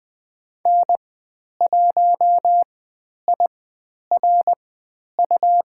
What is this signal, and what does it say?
Morse code, heard as a keyed tone, about 20 wpm, spaced as for 10 wpm, 700 Hz. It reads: N1IRU
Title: 102 Lesson 17g - Letter u - Call Signs 20wpm